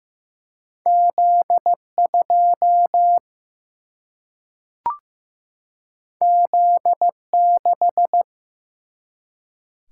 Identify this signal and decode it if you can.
Morse code, heard as a keyed tone, about 15 wpm, 700 Hz. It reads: Z 2 Z 6
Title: Single-Letter Single-Number 15wpm